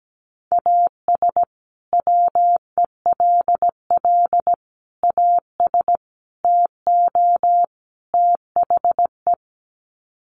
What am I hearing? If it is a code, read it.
Morse code, heard as a keyed tone, about 17 wpm, 700 Hz. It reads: AS WELL AS TO THE
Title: Sets of 5 Words Encore2 17wpm